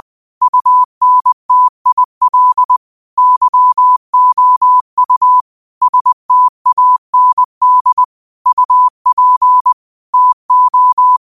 Morse code, heard as a keyed tone, about 20 wpm, 1000 Hz. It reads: UNTIL YOU STAND UP TO
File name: 20wpm